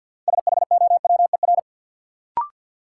Morse code, heard as a keyed tone, about 51 wpm, 700 Hz. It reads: SHOWER
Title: Sentences from Top 1600 Words - New Word Drill Alt - 50wpm